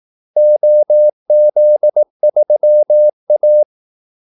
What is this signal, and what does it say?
Morse code, heard as a keyed tone, about 18 wpm, 600 Hz. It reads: OZ3A
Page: Kaldesignaler 18 wpm